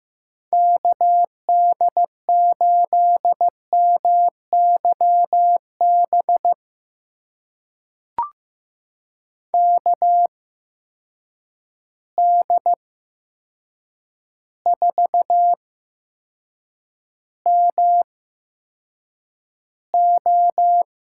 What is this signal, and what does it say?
Morse code, heard as a keyed tone, about 15 wpm, spaced as for 7 wpm, 700 Hz. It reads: KD8MYB KD4MO
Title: USA Call Signs - 6 Characters - 8x Character Spacing - 15wpm